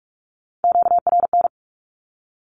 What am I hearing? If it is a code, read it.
Morse code, heard as a keyed tone, about 44 wpm, 700 Hz. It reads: QRN
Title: Q-Codes 45wpm